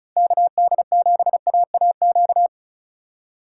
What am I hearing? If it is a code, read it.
Morse code, heard as a keyed tone, about 36 wpm, 700 Hz. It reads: KD7AAQ
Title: USA Call Signs - 6 Characters - 2x Character Spacing - 35wpm